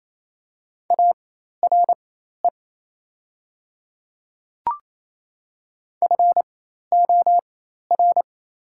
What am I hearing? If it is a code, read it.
Morse code, heard as a keyed tone, about 28 wpm, spaced as for 13 wpm, 690 Hz. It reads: ARE FOR
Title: 4x Character Spacing - Top 100 Words - 28wpm